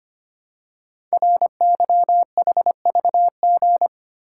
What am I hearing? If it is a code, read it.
Morse code, heard as a keyed tone, about 25 wpm, 700 Hz. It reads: RYHVG
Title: Code Groups - 5 Letters - 25wpm